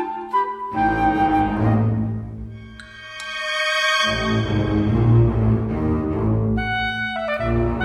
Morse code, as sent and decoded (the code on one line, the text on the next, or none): none